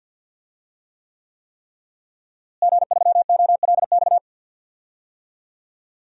Morse code, heard as a keyed tone, about 48 wpm, 700 Hz. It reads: G3OLX
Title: DXCC-100 Call Signs - 5 Characters - Repeated 3x - 50wpm